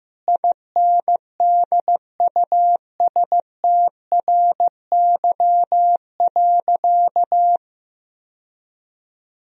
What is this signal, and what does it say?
Morse code, heard as a keyed tone, about 15 wpm, 700 Hz. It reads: INDUSTRY.
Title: Sentences from Top 1000 Words - Review Part 4 - 15wpm